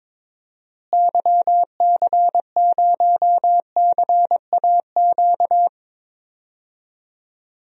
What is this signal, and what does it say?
Morse code, heard as a keyed tone, about 22 wpm, 700 Hz. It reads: YC0CAQ
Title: DXCC-100 Call Signs - 6 Characters - Repeated 2x - 22wpm